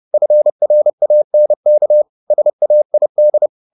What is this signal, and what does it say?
Morse code, heard as a keyed tone, about 30 wpm, 600 Hz. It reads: FRANK SAID